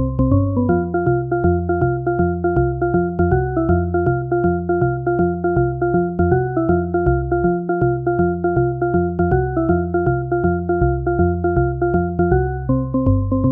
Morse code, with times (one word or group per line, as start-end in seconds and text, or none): none